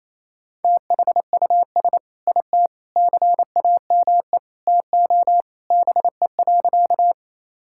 Morse code, heard as a keyed tone, about 28 wpm, 700 Hz. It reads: THUS IT CAME TO BE.
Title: Sentences from Top 1000 Words - Review Part 5 - 28wpm